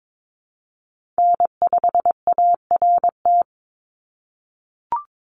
Morse code, heard as a keyed tone, about 22 wpm, 700 Hz. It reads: N5ART
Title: USA Call Signs 22wpm